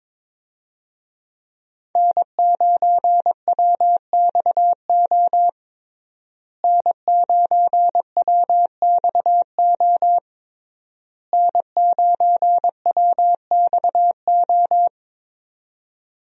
Morse code, heard as a keyed tone, about 21 wpm, 700 Hz. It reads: N9WXO N9WXO N9WXO
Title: USA Call Signs - 5 Characters - Repeated 3x - 22wpm